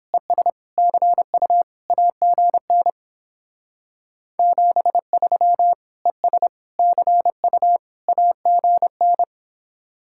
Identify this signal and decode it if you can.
Morse code, heard as a keyed tone, about 31 wpm, 700 Hz. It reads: ES CU AGN 73 ES CU AGN